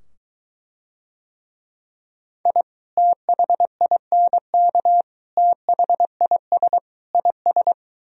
Morse code, heard as a keyed tone, about 23 wpm, 700 Hz. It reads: I THINK THIS IS